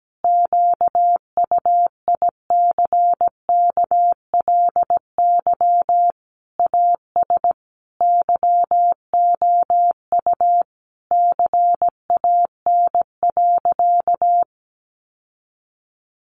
Morse code, heard as a keyed tone, about 17 wpm, 700 Hz. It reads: QUICKLY AS YOU CAN.
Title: Sentences from Top 1100 Words 17wpm